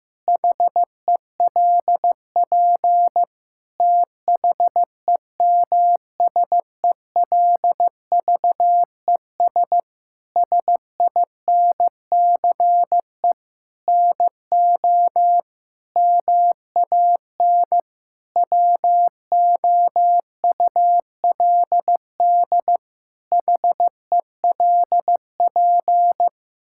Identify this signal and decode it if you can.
Morse code, heard as a keyed tone, about 15 wpm, 700 Hz. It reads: HELP THEMSELVES SINCE NO MAN WOULD HELP
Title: Sentences from Top 1000 Words - Review Part 5 - 15wpm